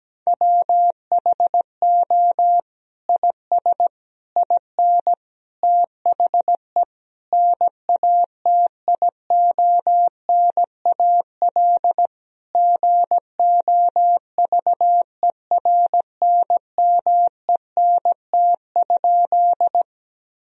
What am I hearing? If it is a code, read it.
Morse code, heard as a keyed tone, about 17 wpm, 700 Hz. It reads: WHO IS IN THE NATIONAL GOVERNMENT?
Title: Sentences from Top 1000 Words - Review Part 4 - 17wpm